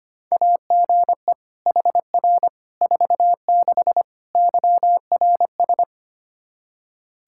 Morse code, heard as a keyed tone, about 25 wpm, 700 Hz. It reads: AGE HR 46 YRS
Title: Practice QSO - Exchange 3 - More Info - 25wpm